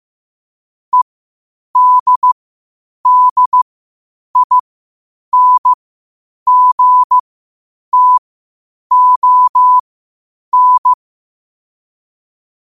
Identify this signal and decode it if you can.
Morse code, heard as a keyed tone, about 15 wpm, spaced as for 8 wpm, 1000 Hz. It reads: EDDINGTON
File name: qotdfiles / 8wpm